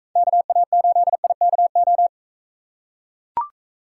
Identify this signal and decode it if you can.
Morse code, heard as a keyed tone, about 42 wpm, 700 Hz. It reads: KA8IKO